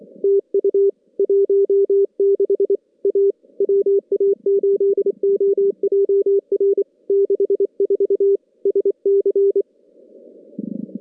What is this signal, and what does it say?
Morse code, heard as a keyed tone, about 25 wpm, 370 Hz. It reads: TU 16 A WA8OJR 64 SC
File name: contests / CW-SS-k5tr- -2100